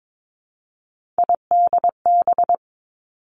Morse code, heard as a keyed tone, about 22 wpm, 700 Hz. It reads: IDB
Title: Three Challenging Letter-Number Combinations 22wpm